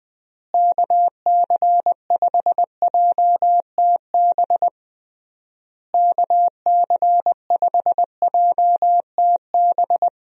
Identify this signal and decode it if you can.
Morse code, heard as a keyed tone, about 20 wpm, 700 Hz. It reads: KC5JTB KC5JTB
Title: USA Call Signs - 6 Characters - Repeated 3x - 20wpm